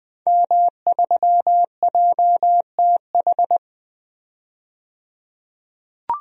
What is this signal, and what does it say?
Morse code, heard as a keyed tone, about 20 wpm, 700 Hz. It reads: M3JTH
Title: DXCC-100 Call Signs - 5 Characters - 20wpm